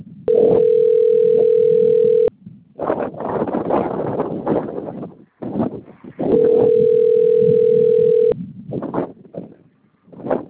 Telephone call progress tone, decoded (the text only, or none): ringback tone